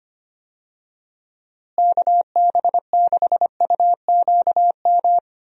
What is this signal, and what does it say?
Morse code, heard as a keyed tone, about 25 wpm, 700 Hz. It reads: KB6UQM
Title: USA Call Signs - 6 Characters - Repeated 2x - 25wpm